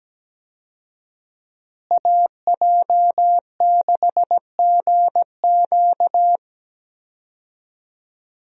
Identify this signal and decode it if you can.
Morse code, heard as a keyed tone, about 17 wpm, 700 Hz. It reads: AJ6GQ
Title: USA Call Signs - 5 Characters - Repeated 3x - 17wpm